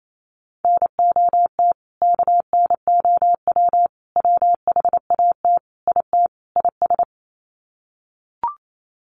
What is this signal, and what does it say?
Morse code, heard as a keyed tone, about 28 wpm, 700 Hz. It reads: NOT KNOW WHAT IT IS
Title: Sets of 5 Words Encore 28wpm